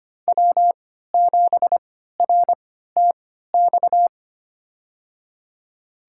Morse code, heard as a keyed tone, about 25 wpm, spaced as for 14 wpm, 700 Hz. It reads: W7RTX
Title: USA Call Signs - 3x Character Spacing - 25wpm